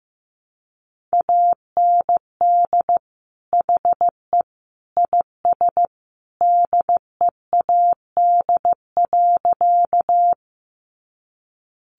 Morse code, heard as a keyed tone, about 15 wpm, 700 Hz. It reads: AND HE IS DEAD.
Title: Sentences from Top 1500 Words 15wpm